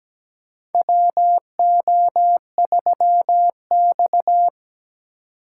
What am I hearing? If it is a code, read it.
Morse code, heard as a keyed tone, about 17 wpm, 700 Hz. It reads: WO3X
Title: USA Call Signs - 4 Characters - 3x Character Spacing - 17wpm